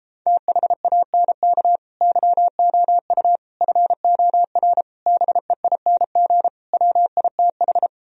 Morse code, heard as a keyed tone, about 31 wpm, 700 Hz. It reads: THANK YOU FOR BEING WITH